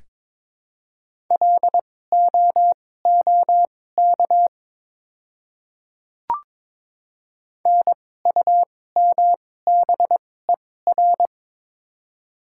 Morse code, heard as a keyed tone, about 22 wpm, spaced as for 16 wpm, 700 Hz. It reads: LOOK NUMBER
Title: 2x Character Spacing - Top 100 Words - 22wpm